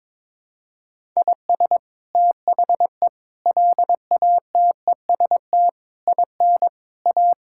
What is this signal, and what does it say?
Morse code, heard as a keyed tone, about 22 wpm, 700 Hz. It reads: IS THE LATEST IN A
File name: practice / 22wpm / Sets of 8 Words - 22wpm